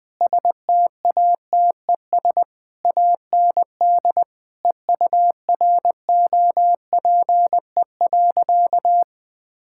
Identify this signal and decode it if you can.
Morse code, heard as a keyed tone, about 19 wpm, 700 Hz. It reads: STATES AND EUROPE.
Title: Sentences with 5-Word Phrases - 20wpm